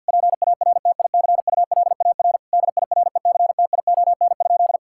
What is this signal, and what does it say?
Morse code, heard as a keyed tone, about 50 wpm, 700 Hz. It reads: PARTICULAR DIRECTION?
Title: Sentences from Top 1000 Words - Review Part 5 - 50wpm